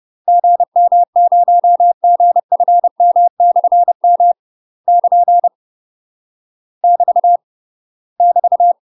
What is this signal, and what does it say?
Morse code, heard as a keyed tone, about 30 wpm, 700 Hz. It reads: GM0GFM/M <KN> <BT> <BT>
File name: UK MASTER QSOs TRAINING/30WPM